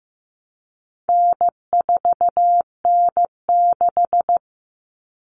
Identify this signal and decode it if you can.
Morse code, heard as a keyed tone, about 15 wpm, 700 Hz. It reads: N4N6
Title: Common Cut Numbers - 4-Digit 15wpm